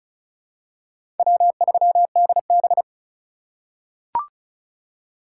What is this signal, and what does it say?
Morse code, heard as a keyed tone, about 36 wpm, 700 Hz. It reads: W3DB
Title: USA Call Signs - 4 Characters - 3x Character Spacing - 35wpm